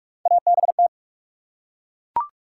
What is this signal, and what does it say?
Morse code, heard as a keyed tone, about 43 wpm, 700 Hz. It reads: ABT
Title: Top 90 QSO Elements 45wpm